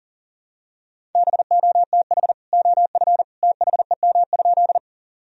Morse code, heard as a keyed tone, about 37 wpm, 700 Hz. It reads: BOTH OF THEM?
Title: Sentences from Top 1000 Words - Review Part 1 - 40wpm